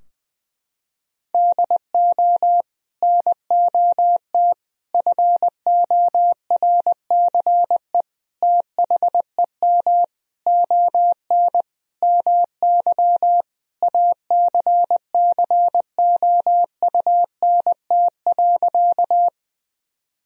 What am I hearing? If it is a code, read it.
Morse code, heard as a keyed tone, about 20 wpm, 700 Hz. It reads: DO NOT FORCE THEM ON MY ACCOUNT.
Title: Sentences from Top 1000 Words - Review Part 3 - 20wpm